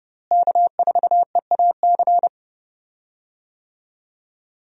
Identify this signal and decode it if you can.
Morse code, heard as a keyed tone, about 31 wpm, 700 Hz. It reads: K4EAC
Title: USA Call Signs - 5 Characters - Repeated 3x - 30wpm